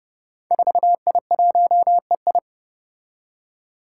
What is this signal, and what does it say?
Morse code, heard as a keyed tone, about 30 wpm, 700 Hz. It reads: 4I1EI
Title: Five Challenging Letter-Number Combinations 30wpm